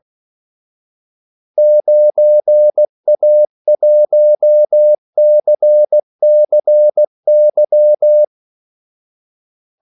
Morse code, heard as a keyed tone, about 16 wpm, 600 Hz. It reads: 9A1CCY
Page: Kaldesignaler 16 wpm